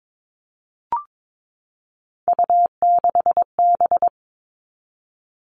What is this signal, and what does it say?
Morse code, heard as a keyed tone, about 22 wpm, 700 Hz. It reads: U6B
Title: All Cut Numbers - 3-Digit 22wpm